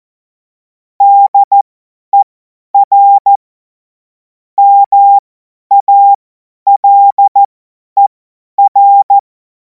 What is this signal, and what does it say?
Morse code, heard as a keyed tone, about 14 wpm, spaced as for 10 wpm, 800 Hz. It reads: DER MALER
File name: Morsekurs / Witze / Witz 14. April 2026 50 F70